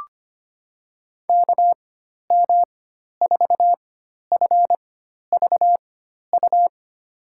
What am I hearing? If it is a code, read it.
Morse code, heard as a keyed tone, about 25 wpm, spaced as for 12 wpm, 700 Hz. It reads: KM4FVU
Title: USA Call Signs - 4x Character Spacing - 25wpm